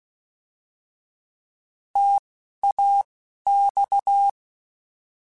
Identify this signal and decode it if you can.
Morse code, heard as a keyed tone, about 16 wpm, spaced as for 11 wpm, 800 Hz. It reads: TAX